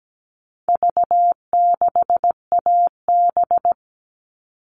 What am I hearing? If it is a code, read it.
Morse code, heard as a keyed tone, about 17 wpm, 700 Hz. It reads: V6AB
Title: All Cut Numbers - 4-Digit 17wpm